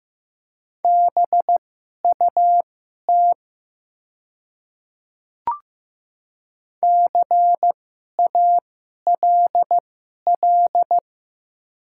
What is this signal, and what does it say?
Morse code, heard as a keyed tone, about 15 wpm, spaced as for 11 wpm, 700 Hz. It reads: BUT CALL
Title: 2x Character Spacing - Top 100 Words - 15wpm